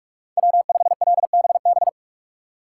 Morse code, heard as a keyed tone, about 46 wpm, 700 Hz. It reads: W5LBB
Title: USA Call Signs - 5 Characters - 2x Character Spacing - 45wpm